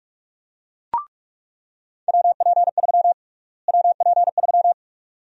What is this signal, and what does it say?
Morse code, heard as a keyed tone, about 46 wpm, 700 Hz. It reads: WP3 WP3
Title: USA Call Sign Prefixes - Repeated 2x - 45wpm